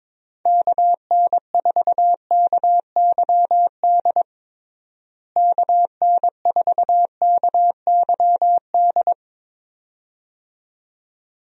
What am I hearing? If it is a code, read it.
Morse code, heard as a keyed tone, about 23 wpm, 700 Hz. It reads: KN4KYD KN4KYD
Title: USA Call Signs - 6 Characters - Repeated 2x - 22wpm